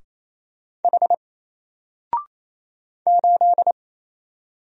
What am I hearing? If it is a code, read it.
Morse code, heard as a keyed tone, about 28 wpm, 700 Hz. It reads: H 8
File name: Single-Letter-Number-28wpm